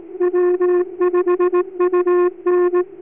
Morse code, heard as a keyed tone, about 18 wpm, 360 Hz. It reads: W5UN